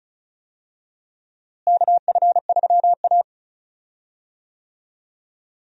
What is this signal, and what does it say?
Morse code, heard as a keyed tone, about 34 wpm, 700 Hz. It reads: KF3A
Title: USA Call Signs - 4 Characters - Repeated 2x - 35wpm